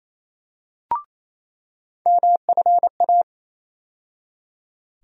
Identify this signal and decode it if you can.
Morse code, heard as a keyed tone, about 28 wpm, 700 Hz. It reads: MFA
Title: USA Call Sign Postfixes - 3 Letters 28wpm